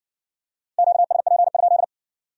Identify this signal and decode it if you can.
Morse code, heard as a keyed tone, about 56 wpm, 700 Hz. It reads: QSP?